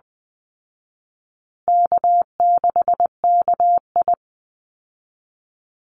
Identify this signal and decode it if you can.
Morse code, heard as a keyed tone, about 20 wpm, 700 Hz. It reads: K6KI
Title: Call Signs Repeated 2x - 20wpm